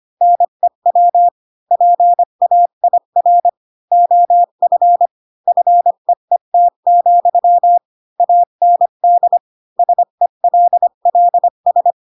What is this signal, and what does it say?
Morse code, heard as a keyed tone, about 25 wpm, 700 Hz. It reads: NEW PAIR OF FEET, AND SELLS
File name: Pinocchio / Pinocchio ch8 20 25 ik1hge